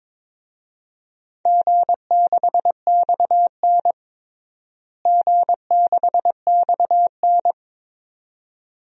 Morse code, heard as a keyed tone, about 22 wpm, 700 Hz. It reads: G6XN G6XN
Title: DXCC-100 Call Signs - 4 Characters - Repeated 2x - 22wpm